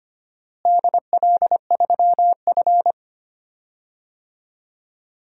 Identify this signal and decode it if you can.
Morse code, heard as a keyed tone, about 25 wpm, 700 Hz. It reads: DL3F
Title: DXCC-100 Call Signs - 4 Characters - Repeated 2x - 25wpm